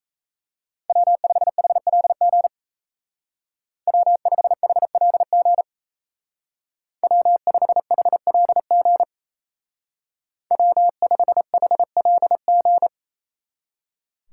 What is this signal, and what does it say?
Morse code, heard as a keyed tone, about 36 wpm, 700 Hz. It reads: W5HLG W5HLG W5HLG W5HLG